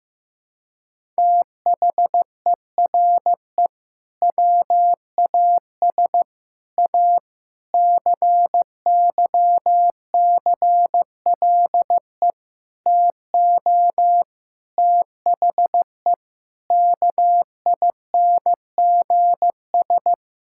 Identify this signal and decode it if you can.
Morse code, heard as a keyed tone, about 15 wpm, 700 Hz. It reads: THERE WAS A CYCLE TO THE KINGS
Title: Sentences from Top 1000 Words - Review Part 6 - 15wpm